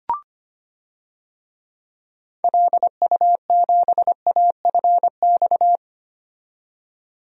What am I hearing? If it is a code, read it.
Morse code, heard as a keyed tone, about 25 wpm, 700 Hz. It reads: LU7AFX
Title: DXCC-100 Call Signs - 6 Characters - 25wpm